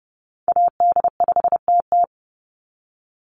Morse code, heard as a keyed tone, about 30 wpm, 700 Hz. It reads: AD5TT
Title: Call Signs - 30wpm